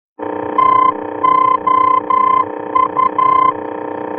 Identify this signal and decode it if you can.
Morse code, heard as a keyed tone, about 11 wpm, 1030 Hz, with noise. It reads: TOU
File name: france / mx / TOU-F-VOR